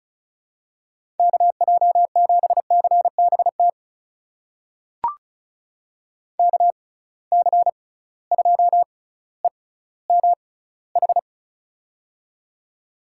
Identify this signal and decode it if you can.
Morse code, heard as a keyed tone, about 36 wpm, spaced as for 18 wpm, 700 Hz. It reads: KJ7CBT KC2EMH
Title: USA Call Signs - 6 Characters - 6x Character Spacing - 35wpm